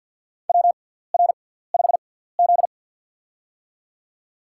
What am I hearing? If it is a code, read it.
Morse code, heard as a keyed tone, about 53 wpm, spaced as for 17 wpm, 700 Hz. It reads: WR5Z
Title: USA Call Signs - 4 Characters - 6x Character Spacing - 50wpm